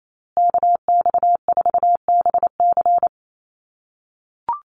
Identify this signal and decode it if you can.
Morse code, heard as a keyed tone, about 28 wpm, 700 Hz. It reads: KX4BC
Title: USA Call Signs Repeated 2x 28wpm